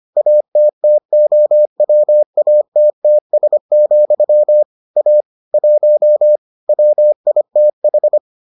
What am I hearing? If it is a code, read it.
Morse code, heard as a keyed tone, about 25 wpm, 600 Hz. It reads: ATTOWATTS, A 1 WITH